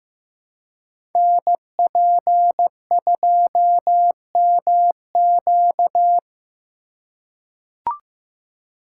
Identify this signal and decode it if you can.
Morse code, heard as a keyed tone, about 15 wpm, 700 Hz. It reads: NP2MQ
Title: Code Groups - 5 Letter:Numbers - 15wpm